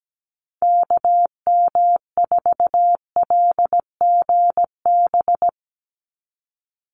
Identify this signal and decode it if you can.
Morse code, heard as a keyed tone, about 17 wpm, 700 Hz. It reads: KM4LGB
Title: USA Call Signs 17wpm